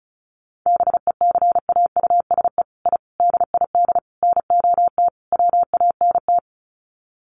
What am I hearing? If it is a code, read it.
Morse code, heard as a keyed tone, about 34 wpm, 700 Hz. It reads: BECAUSE I DID NOT WANT
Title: Sets of 5 Words Encore2 35wpm